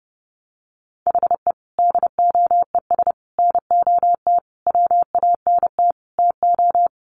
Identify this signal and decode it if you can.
Morse code, heard as a keyed tone, about 30 wpm, 700 Hz. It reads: HE DOES NOT WANT TO
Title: Sets of 5 Words 30wpm